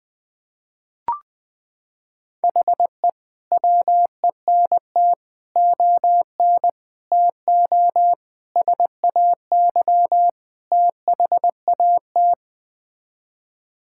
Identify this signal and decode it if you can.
Morse code, heard as a keyed tone, about 20 wpm, 700 Hz. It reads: HE WENT ON TO SAY THAT
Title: Sets of 6 Words - 20wpm